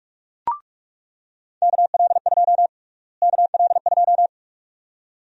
Morse code, heard as a keyed tone, about 46 wpm, 700 Hz. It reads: KL2 KL2
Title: USA Call Sign Prefixes - Repeated 2x - 45wpm